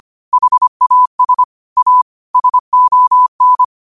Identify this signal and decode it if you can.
Morse code, heard as a keyed tone, about 25 wpm, 1000 Hz. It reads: HAS A SON